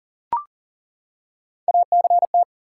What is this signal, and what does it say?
Morse code, heard as a keyed tone, about 40 wpm, 700 Hz. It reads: ACT
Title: Three Letter Words - 40wpm